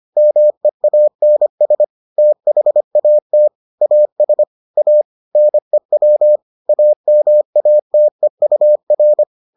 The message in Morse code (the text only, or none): MEANS THAT AS A NEW AMATEUR